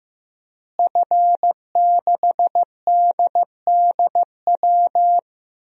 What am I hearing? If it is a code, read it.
Morse code, heard as a keyed tone, about 16 wpm, 700 Hz. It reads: F6DDW
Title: DXCC-100 Call Signs - 5 Characters - Repeated 2x - 15wpm